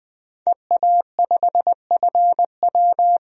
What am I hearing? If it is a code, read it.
Morse code, heard as a keyed tone, about 20 wpm, 700 Hz. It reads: EA5FW
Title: DXCC-100 Call Signs - 5 Characters - 20wpm